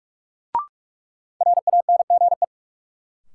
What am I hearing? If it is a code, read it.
Morse code, heard as a keyed tone, about 47 wpm, 700 Hz. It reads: RANGE
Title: Sets of 4 Words - New Word Drill - 45wpm